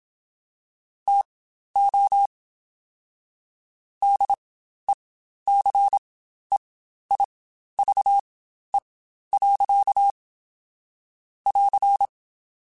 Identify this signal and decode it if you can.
Morse code, heard as a keyed tone, about 27 wpm, spaced as for 12 wpm, 800 Hz. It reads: TO DECEIVE. <AR>
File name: Quotes Spaced / Quotes Spaced-mp3s / Deceive sp @25